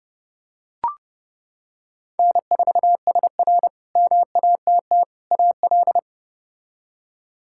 Morse code, heard as a keyed tone, about 29 wpm, 700 Hz. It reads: N4SR MATT AL
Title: K1USN SST QSO Practice - 30wpm